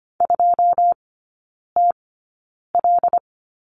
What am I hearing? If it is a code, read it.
Morse code, heard as a keyed tone, about 25 wpm, spaced as for 9 wpm, 700 Hz. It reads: 2TL
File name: Lesson 17e - Letter u - Three Characters 25wpm